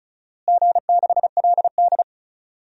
Morse code, heard as a keyed tone, about 36 wpm, 700 Hz. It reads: G6LD
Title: DXCC-100 Call Signs - 4 Characters - Repeated 2x - 35wpm